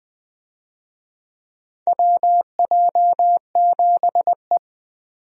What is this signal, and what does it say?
Morse code, heard as a keyed tone, about 20 wpm, 700 Hz. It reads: WJ7E
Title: USA Call Signs - 4 Characters - 20wpm